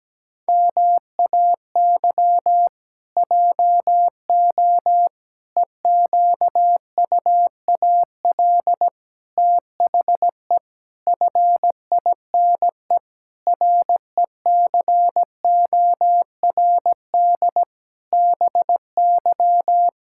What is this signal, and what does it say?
Morse code, heard as a keyed tone, about 17 wpm, 700 Hz. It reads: MAY JO EQUAL THE FINE RECORD BY
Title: Pangram Challenge v2 - 17wpm